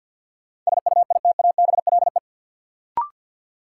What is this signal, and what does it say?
Morse code, heard as a keyed tone, about 50 wpm, 700 Hz. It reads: SUITABLE
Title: Sentences from Top 1400 Words - New Word Drill Alt - 50wpm